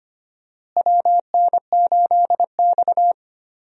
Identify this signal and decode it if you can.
Morse code, heard as a keyed tone, about 25 wpm, 700 Hz. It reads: WN8X
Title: USA Call Signs - 4 Characters - Repeated 3x - 25wpm